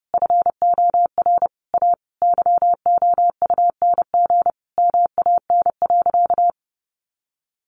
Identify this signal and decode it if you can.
Morse code, heard as a keyed tone, about 28 wpm, 700 Hz. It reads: FOR A YOUNG MAN.
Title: Sentences from Top 800 Words 30wpm